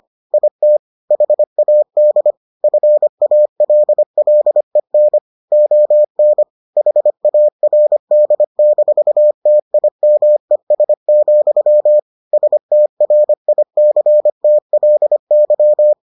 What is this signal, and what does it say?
Morse code, heard as a keyed tone, about 25 wpm, 600 Hz. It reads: IT HAD FALLEN ON HARD-TIMES, STRICTLY